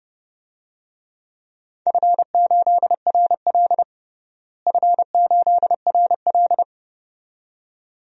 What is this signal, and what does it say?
Morse code, heard as a keyed tone, about 31 wpm, 700 Hz. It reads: F8RL F8RL
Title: DXCC-100 Call Signs - 4 Characters - Repeated 2x - 30wpm